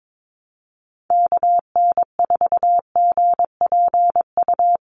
Morse code, heard as a keyed tone, about 22 wpm, 700 Hz. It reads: KN4GPU
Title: USA Call Signs 22wpm